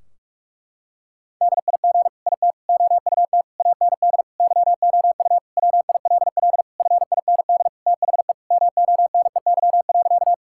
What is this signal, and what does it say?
Morse code, heard as a keyed tone, about 41 wpm, 700 Hz. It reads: DIG IT OUT AND YOU WILL FIND THE MONEY.
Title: Sentences from Top 1600 Words 45wpm